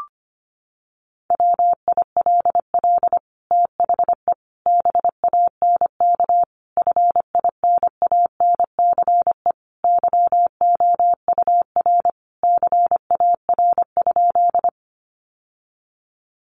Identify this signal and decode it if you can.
Morse code, heard as a keyed tone, about 25 wpm, 700 Hz. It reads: WILL THE BANK FINANCE YOUR CAR?
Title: Sentences from Top 1500 Words 25wpm